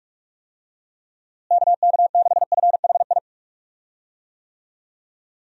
Morse code, heard as a keyed tone, about 46 wpm, 700 Hz. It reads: KK6FHI
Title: USA Call Signs - 6 Characters - Repeated 2x - 45wpm